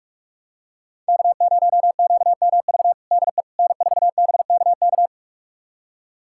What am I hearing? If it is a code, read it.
Morse code, heard as a keyed tone, about 41 wpm, 700 Hz. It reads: K0QMV DE N4BKK
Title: USA Callsign DE Callsign - 5 Characters - 45wpm